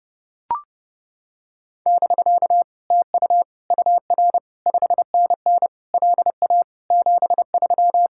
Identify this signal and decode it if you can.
Morse code, heard as a keyed tone, about 29 wpm, 700 Hz. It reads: <BK> TU UR 5NN LA 73